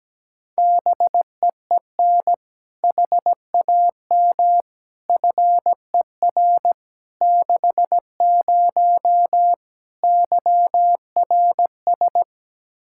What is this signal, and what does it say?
Morse code, heard as a keyed tone, about 17 wpm, 700 Hz. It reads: BEEN HAM FER 60 YRS
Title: Practice QSO - Exchange 3 - More Info - 17wpm